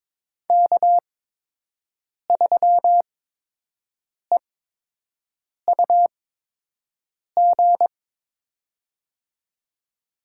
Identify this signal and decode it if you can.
Morse code, heard as a keyed tone, about 22 wpm, spaced as for 6 wpm, 700 Hz. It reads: K3EUG
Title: USA Call Signs - 5 Characters - 8x Character Spacing - 22wpm